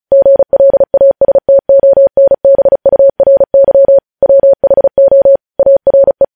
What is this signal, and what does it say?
Morse code, heard as a keyed tone, about 36 wpm, 550 Hz. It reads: GLASTONBURY WHO ARE
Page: News Headlines in Morse Code at 35 WPM